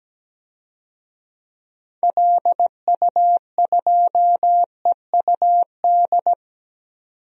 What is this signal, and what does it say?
Morse code, heard as a keyed tone, about 17 wpm, 700 Hz. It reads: LU2EUD
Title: DXCC-100 Call Signs - 6 Characters - 17wpm